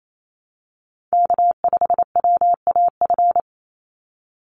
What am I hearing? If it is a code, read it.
Morse code, heard as a keyed tone, about 28 wpm, 700 Hz. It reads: K5WAF
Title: Call Signs Repeated 2x - 28wpm